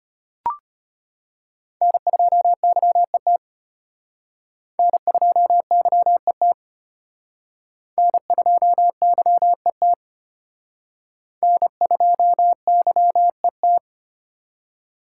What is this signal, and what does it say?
Morse code, heard as a keyed tone, about 31 wpm, 700 Hz. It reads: N2YET N2YET N2YET N2YET
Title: Speed Racing - USA Call Signs 25wpm